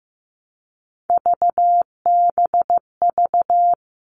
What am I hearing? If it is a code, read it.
Morse code, heard as a keyed tone, about 15 wpm, 700 Hz. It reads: VBV
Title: All Cut Numbers - 3-Digit 15wpm